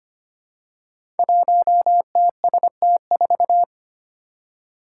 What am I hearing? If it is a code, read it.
Morse code, heard as a keyed tone, about 25 wpm, 700 Hz. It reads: 1TST4
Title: Five Challenging Letter-Number Combinations 25wpm